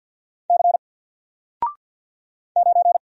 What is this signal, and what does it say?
Morse code, heard as a keyed tone, about 49 wpm, 700 Hz. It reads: C 9
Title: Single Letter-Number Alt 50wpm